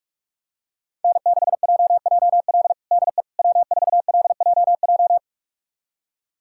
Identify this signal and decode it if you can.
Morse code, heard as a keyed tone, about 41 wpm, 700 Hz. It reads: N6JJL DE W4LJJ
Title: USA Callsign DE Callsign - 5 Characters - 45wpm